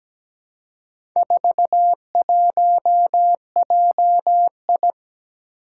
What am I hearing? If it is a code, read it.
Morse code, heard as a keyed tone, about 17 wpm, 700 Hz. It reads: 41JI
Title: Four Challenging Letter-Number Combinations 17wpm